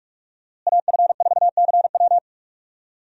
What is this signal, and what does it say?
Morse code, heard as a keyed tone, about 46 wpm, 700 Hz. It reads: AF4CW
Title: USA Call Signs - 5 Characters - Repeated 2x - 45wpm